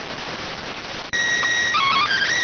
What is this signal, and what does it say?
SSTV (slow-scan television), PD 290.